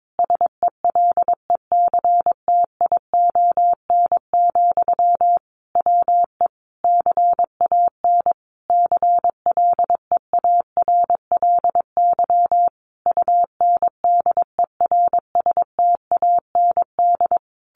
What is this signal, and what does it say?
Morse code, heard as a keyed tone, about 22 wpm, 700 Hz. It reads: SELECTION, WE CAN CLEARLY UNDERSTAND
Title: Sentences from Top 1400 Words 22wpm